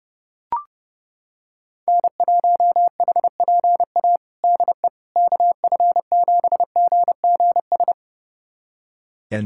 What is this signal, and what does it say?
Morse code, heard as a keyed tone, about 30 wpm, 700 Hz. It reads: N1HPA DE KF7GGS
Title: USA Callsign DE Callsign 30wpm